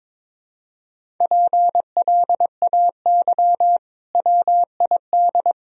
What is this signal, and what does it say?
Morse code, heard as a keyed tone, about 21 wpm, 700 Hz. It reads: PLAY WID
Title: Sentences from Top 200 Words and Abbreviations 22wpm